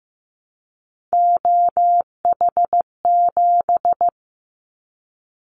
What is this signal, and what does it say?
Morse code, heard as a keyed tone, about 15 wpm, 700 Hz. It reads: OH7
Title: Callsign Prefixes for Top 50 DXCC - 15wpm